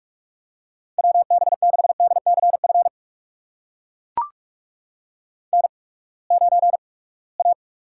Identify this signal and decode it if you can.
Morse code, heard as a keyed tone, about 44 wpm, spaced as for 25 wpm, 700 Hz. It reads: WB6DCF N9A
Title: USA Call Signs - 8x Character Spacing - 45wpm